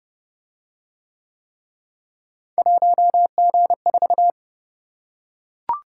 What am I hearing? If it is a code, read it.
Morse code, heard as a keyed tone, about 29 wpm, 700 Hz. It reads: 1G4